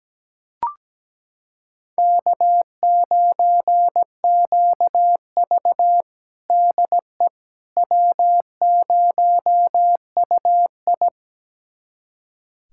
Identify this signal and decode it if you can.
Morse code, heard as a keyed tone, about 17 wpm, 700 Hz. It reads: K9QV DE W0UI
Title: USA Callsign DE Callsign - 4 Characters - 17wpm